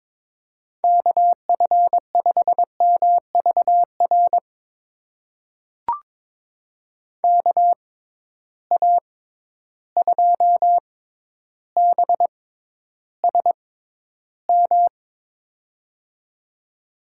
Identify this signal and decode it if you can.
Morse code, heard as a keyed tone, about 23 wpm, spaced as for 11 wpm, 700 Hz. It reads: KF5MVR KA2BSM
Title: USA Call Signs - 6 Characters - 6x Character Spacing - 22wpm